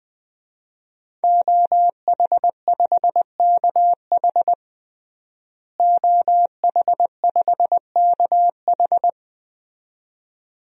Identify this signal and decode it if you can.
Morse code, heard as a keyed tone, about 21 wpm, 700 Hz. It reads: OH5KH OH5KH
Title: DXCC-100 Call Signs - 5 Characters - Repeated 2x - 20wpm